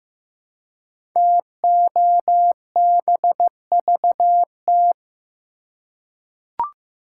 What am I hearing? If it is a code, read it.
Morse code, heard as a keyed tone, about 15 wpm, 700 Hz. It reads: TOBVT
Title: Code Groups - 5 Letters - 15wpm